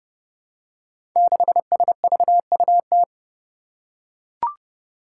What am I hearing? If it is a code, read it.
Morse code, heard as a keyed tone, about 28 wpm, 700 Hz. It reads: 6SVUT
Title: Five Challenging Letter-Number Combinations 30wpm